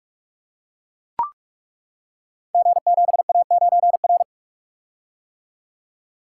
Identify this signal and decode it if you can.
Morse code, heard as a keyed tone, about 42 wpm, 700 Hz. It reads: G7A9R
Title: Code Groups - 5 Letter:Numbers - 45wpm